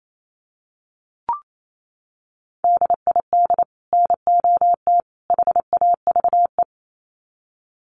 Morse code, heard as a keyed tone, about 28 wpm, 700 Hz. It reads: DID NOT HAVE
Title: Sets of 3 Words 28wpm